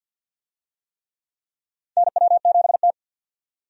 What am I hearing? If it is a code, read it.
Morse code, heard as a keyed tone, about 51 wpm, 700 Hz. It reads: NW7T
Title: USA Call Signs - 4 Characters - 50wpm